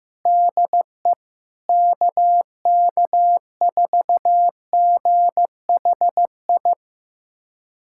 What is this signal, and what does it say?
Morse code, heard as a keyed tone, about 16 wpm, 700 Hz. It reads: DE KK4GHI
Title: USA Callsign DE Callsign 15wpm